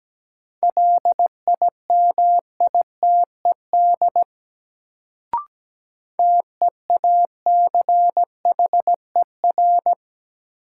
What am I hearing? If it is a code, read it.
Morse code, heard as a keyed tone, about 17 wpm, 700 Hz. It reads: LIMITED TEACHER
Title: Sentences from Top 900 Words - New Word Drill - 17wpm